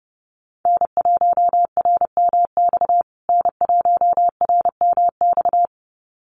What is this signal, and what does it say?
Morse code, heard as a keyed tone, about 30 wpm, 700 Hz. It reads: N1RMX N1RMX
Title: USA Call Signs Repeated 2x 30wpm